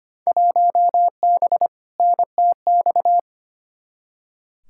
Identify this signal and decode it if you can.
Morse code, heard as a keyed tone, about 25 wpm, 700 Hz. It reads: 1B NTX
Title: Field Day Exchanges - 25wpm